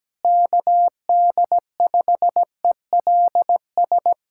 Morse code, heard as a keyed tone, about 17 wpm, 700 Hz. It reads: KD5ELS
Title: USA Callsign DE Callsign 17wpm